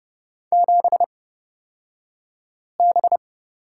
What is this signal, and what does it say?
Morse code, heard as a keyed tone, about 30 wpm, spaced as for 5 wpm, 700 Hz. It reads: 7B
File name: Lesson 37d - Letter x - Two Characters 30wpm